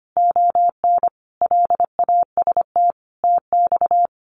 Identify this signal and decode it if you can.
Morse code, heard as a keyed tone, about 25 wpm, 700 Hz. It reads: ON LAST TX
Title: Common QSO Phrases 25wpm